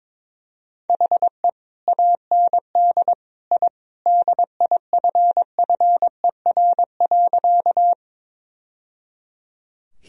Morse code, heard as a keyed tone, about 22 wpm, 700 Hz. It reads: HE AND I DIFFER.
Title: Sentences from Top 1000 Words - Review Part 1 - 22wpm